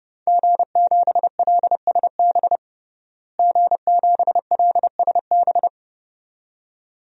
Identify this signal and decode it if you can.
Morse code, heard as a keyed tone, about 30 wpm, 700 Hz. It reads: G7LSB G7LSB
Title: DXCC-100 Call Signs - 5 Characters - Repeated 2x - 30wpm